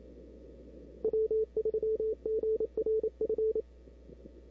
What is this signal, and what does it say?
Morse code, heard as a keyed tone, about 27 wpm, 440 Hz. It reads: W3GRF
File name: CW / 2005-k5tr arrl cw dx -03:00